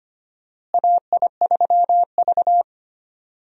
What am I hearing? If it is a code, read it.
Morse code, heard as a keyed tone, about 25 wpm, 700 Hz. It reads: AI3V
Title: USA Call Signs - 4 Characters - Repeated 2x - 25wpm